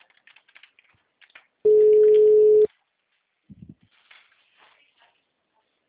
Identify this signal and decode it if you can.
Ringback tone (Europe, 425 Hz).
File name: out- -2005- - - .2